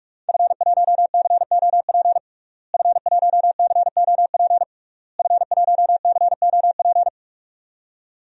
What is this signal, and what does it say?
Morse code, heard as a keyed tone, about 46 wpm, 700 Hz. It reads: F1COP F1COP F1COP
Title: DXCC-100 Call Signs - 5 Characters - Repeated 3x - 45wpm